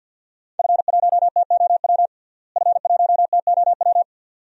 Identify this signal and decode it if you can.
Morse code, heard as a keyed tone, about 52 wpm, 700 Hz. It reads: F1TOW F1TOW
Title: DXCC-100 Call Signs - 5 Characters - Repeated 2x - 50wpm